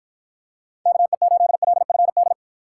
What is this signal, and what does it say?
Morse code, heard as a keyed tone, about 53 wpm, 700 Hz. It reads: KE8LFD